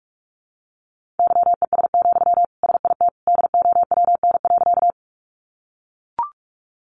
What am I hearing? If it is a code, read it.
Morse code, heard as a keyed tone, about 44 wpm, 700 Hz. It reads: YES, SIT DOWN.
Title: Sentences from Top 500 Words 45wpm